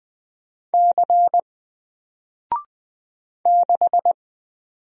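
Morse code, heard as a keyed tone, about 20 wpm, 700 Hz. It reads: C 6
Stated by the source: Single letter and number Morse drill at 20 wpm, from N4LCD.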